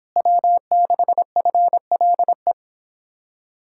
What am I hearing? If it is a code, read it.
Morse code, heard as a keyed tone, about 26 wpm, 700 Hz. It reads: W6FLE